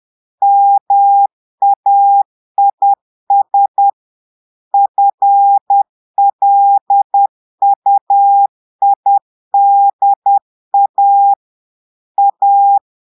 Morse code, heard as a keyed tone, about 10 wpm, 800 Hz. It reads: MAIS FLUIDA A
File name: Portugais - texte en clair 1 - vitesse - 10 mots minute